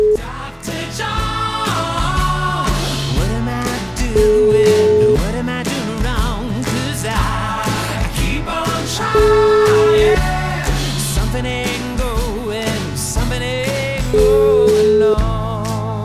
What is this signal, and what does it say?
Ringback tone (Europe, 425 Hz), with noise.